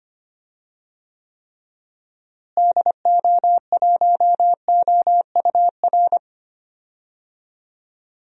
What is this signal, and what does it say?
Morse code, heard as a keyed tone, about 25 wpm, 700 Hz. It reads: DO1OUR
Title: DXCC-100 Call Signs - 6 Characters - Repeated 2x - 25wpm